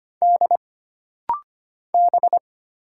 Morse code, heard as a keyed tone, about 25 wpm, 700 Hz. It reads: D B
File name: Single Character (Letter:Number:Punctuation:Prosign) - Rapid-Fire - 25wpm